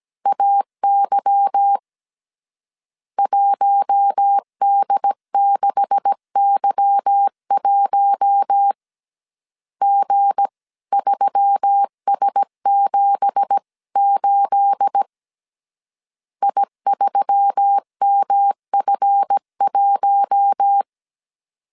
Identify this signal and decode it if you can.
Morse code, heard as a keyed tone, about 17 wpm, 800 Hz. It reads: AY 1D6Y1 G3S78 I3MF1
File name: CW / DET / 1 9 1500C S C 16wpm 2909